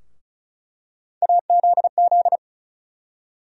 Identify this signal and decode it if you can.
Morse code, heard as a keyed tone, about 34 wpm, 700 Hz. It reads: AZZ
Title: USA Call Sign Postfixes - 3 Letters 35wpm